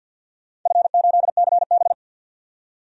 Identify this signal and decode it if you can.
Morse code, heard as a keyed tone, about 48 wpm, 700 Hz. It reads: F8CB
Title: DXCC-100 Call Signs - 4 Characters - 50wpm